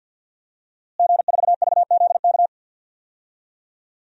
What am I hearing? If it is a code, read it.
Morse code, heard as a keyed tone, about 52 wpm, 700 Hz. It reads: G4VZK
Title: DXCC-100 Call Signs - 5 Characters - Repeated 2x - 50wpm